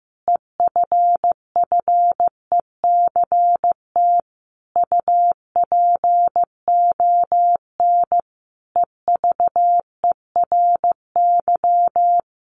Morse code, heard as a keyed tone, about 15 wpm, 700 Hz. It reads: EFFECT UPON EVERY
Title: Sentences from Top 1100 Words 15wpm